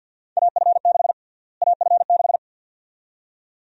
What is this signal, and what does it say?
Morse code, heard as a keyed tone, about 53 wpm, 700 Hz. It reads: AF6 AF6
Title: USA Call Sign Prefixes - Repeated 2x - 50wpm